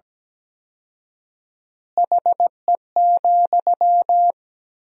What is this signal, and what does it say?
Morse code, heard as a keyed tone, about 17 wpm, 700 Hz. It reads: HE,
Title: 4x Word-Spacing - Sentences from Top 500 Words - 17wpm